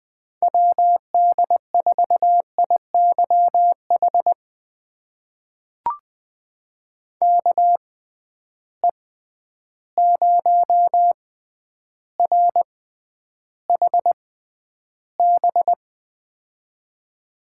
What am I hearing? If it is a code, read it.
Morse code, heard as a keyed tone, about 20 wpm, spaced as for 10 wpm, 700 Hz. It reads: WD4IYH KE0RHB
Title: USA Call Signs - 6 Characters - 6x Character Spacing - 20wpm